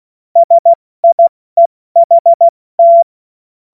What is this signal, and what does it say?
Morse code, heard as a keyed tone, about 16 wpm, 680 Hz. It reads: SIEHT
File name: Morsekurs / Witze / Witz 04. Juli 2023 70 F80